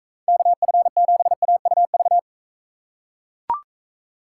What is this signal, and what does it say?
Morse code, heard as a keyed tone, about 42 wpm, 700 Hz. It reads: KF7AUV